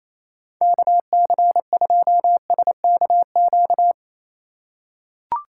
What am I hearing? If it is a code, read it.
Morse code, heard as a keyed tone, about 28 wpm, 700 Hz. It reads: KC2SKQ
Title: USA Call Signs - 6 Characters - 8x Character Spacing - 28wpm